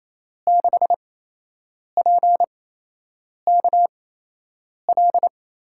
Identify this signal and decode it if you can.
Morse code, heard as a keyed tone, about 28 wpm, spaced as for 8 wpm, 700 Hz. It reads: 6PKL
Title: USA Call Signs - 6 Characters - 8x Character Spacing - 28wpm